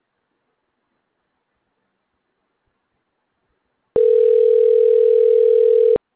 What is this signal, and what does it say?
Ringback tone (North America), 440 + 480 Hz.